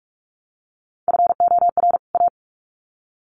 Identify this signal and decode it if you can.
Morse code, heard as a keyed tone, about 44 wpm, 700 Hz. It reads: FOR A
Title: Sets of 2 Words 45wpm